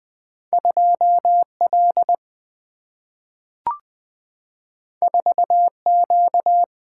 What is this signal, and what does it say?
Morse code, heard as a keyed tone, about 20 wpm, 700 Hz. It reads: 2 L 4 Q
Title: Single-Letter Single-Number 20wpm